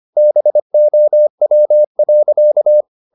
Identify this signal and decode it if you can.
Morse code, heard as a keyed tone, about 25 wpm, 600 Hz. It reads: BOW.